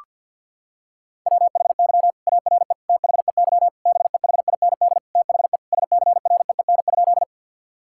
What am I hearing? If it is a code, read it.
Morse code, heard as a keyed tone, about 52 wpm, 700 Hz. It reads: WHY ARE THEY BEHIND THE SCREEN?
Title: Sentences from Top 1000 Words - Review Part 6 - 50wpm